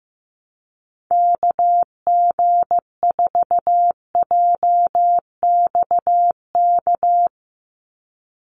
Morse code, heard as a keyed tone, about 15 wpm, 700 Hz. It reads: KG4JXK
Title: USA Call Signs 15wpm